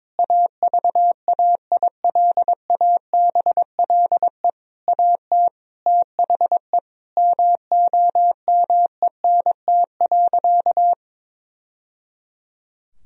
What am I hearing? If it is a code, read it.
Morse code, heard as a keyed tone, about 22 wpm, 700 Hz. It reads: AVAILABLE AT THE MOMENT.
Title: Sentences from Top 1000 Words - Review Part 3 - 22wpm